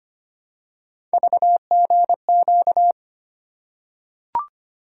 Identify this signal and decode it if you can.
Morse code, heard as a keyed tone, about 25 wpm, 700 Hz. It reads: VGQ
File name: Three Letter-Number Combinations - 25wpm